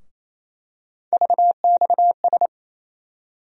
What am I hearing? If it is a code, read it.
Morse code, heard as a keyed tone, about 28 wpm, 700 Hz. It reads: VXS
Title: USA Call Sign Postfixes - 3 Letters 28wpm